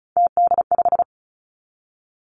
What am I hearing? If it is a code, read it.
Morse code, heard as a keyed tone, about 36 wpm, 700 Hz. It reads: TD5
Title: Three Challenging Letter-Number Combinations 35wpm